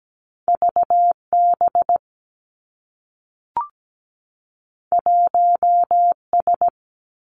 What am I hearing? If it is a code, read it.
Morse code, heard as a keyed tone, about 17 wpm, 700 Hz. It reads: V B 1 S
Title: Two Challenging Letter-Number Combinations 17wpm